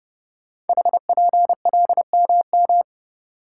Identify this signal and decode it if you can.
Morse code, heard as a keyed tone, about 28 wpm, 690 Hz. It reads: HPLMM
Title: Code Groups - 5 Letters - 30wpm